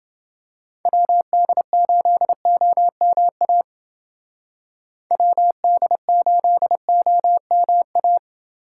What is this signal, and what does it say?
Morse code, heard as a keyed tone, about 28 wpm, 700 Hz. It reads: WD8OMA WD8OMA